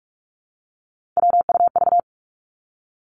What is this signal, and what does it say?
Morse code, heard as a keyed tone, about 46 wpm, 700 Hz. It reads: WUV
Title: Three Challenging Letter-Number Combinations 45wpm